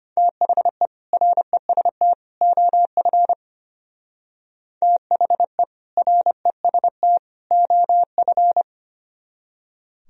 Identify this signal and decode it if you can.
Morse code, heard as a keyed tone, about 32 wpm, 700 Hz. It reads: THE REST OF THE REST OF